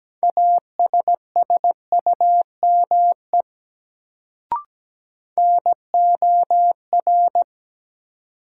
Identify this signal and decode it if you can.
Morse code, heard as a keyed tone, about 17 wpm, 700 Hz. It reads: ASSUME NOR
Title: Sentences from Top 1500 Words - New Word Drill - 17wpm